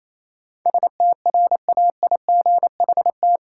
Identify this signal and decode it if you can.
Morse code, heard as a keyed tone, about 28 wpm, 700 Hz. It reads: STRAIGHT
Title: Sentences from Top 1000 Words - New Word Drill - 28wpm